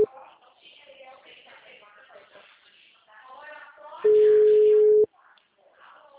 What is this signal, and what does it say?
Ringback tone (Europe, 425 Hz), with speech.